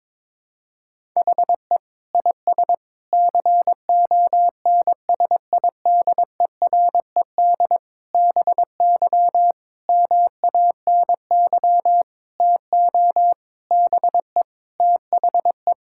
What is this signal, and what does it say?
Morse code, heard as a keyed tone, about 22 wpm, 700 Hz. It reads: HE IS CONSIDERED BY MANY TO BE THE
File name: Sentences with 7-Word Phrases - 22wpm